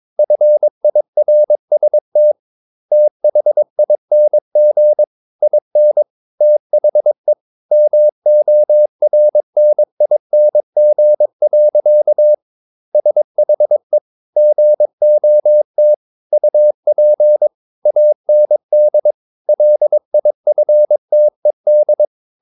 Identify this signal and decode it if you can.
Morse code, heard as a keyed tone, about 22 wpm, 600 Hz. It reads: FIRST THING IN THE MORNING. SHE GOT UP AND LIFTED
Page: Romanen Never af Ken Follet på engelsk.